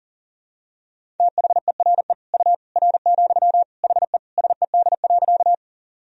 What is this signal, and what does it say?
Morse code, heard as a keyed tone, about 40 wpm, 700 Hz. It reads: THERE U R, HE SED.
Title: Sentences from Top 100 Words and Abbreviations 40wpm